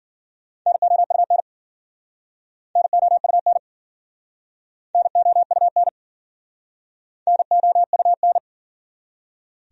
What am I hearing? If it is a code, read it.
Morse code, heard as a keyed tone, about 53 wpm, 700 Hz. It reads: NOUN NOUN NOUN NOUN